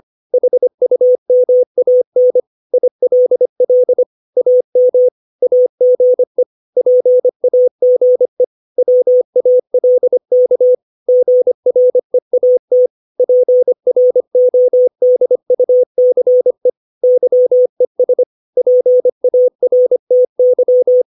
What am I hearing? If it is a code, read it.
Morse code, heard as a keyed tone, about 25 wpm, 500 Hz. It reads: HUMAN ILL AM AGE PAGE WALK GREAT PRODUCE YES PARTY
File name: 632 words / 632 Words 25WPM